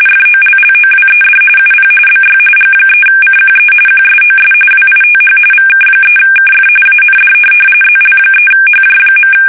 RTTY, about 45.45 baud. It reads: FOLLOWED UP BY G3CQE QSOS WITH VK3KF AND ZL3HJ.
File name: fsk850 / teletype-RTTY